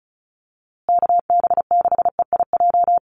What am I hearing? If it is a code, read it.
Morse code, heard as a keyed tone, about 34 wpm, 700 Hz. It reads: KB6EIJ
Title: USA Call Signs Repeated 2x 35wpm